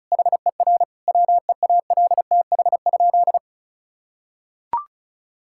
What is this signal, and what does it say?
Morse code, heard as a keyed tone, about 34 wpm, 700 Hz. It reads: HER WEALTH?
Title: Sentences from Top 1600 Words 35wpm